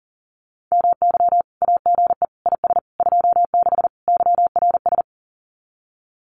Morse code, heard as a keyed tone, about 40 wpm, 700 Hz. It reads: MY AGE IS 26 YRS
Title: Common QSO Phrases Encore 40wpm